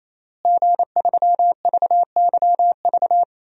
Morse code, heard as a keyed tone, about 28 wpm, 700 Hz. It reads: G3VYV
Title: DXCC-100 Call Signs - 5 Characters - Repeated 3x - 28wpm